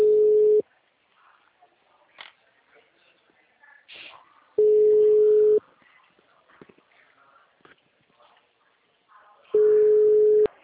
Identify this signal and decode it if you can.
Ringback tone (Europe, 425 Hz).
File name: out- -2008- - - .26249